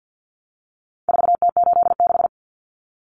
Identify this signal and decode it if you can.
Morse code, heard as a keyed tone, about 50 wpm, 700 Hz. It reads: 4T86
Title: Common Cut Numbers - 4-Digit 50wpm